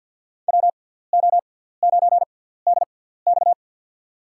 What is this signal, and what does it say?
Morse code, heard as a keyed tone, about 53 wpm, spaced as for 17 wpm, 700 Hz. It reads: WO9DX
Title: USA Call Signs - 5 Characters - 6x Character Spacing - 50wpm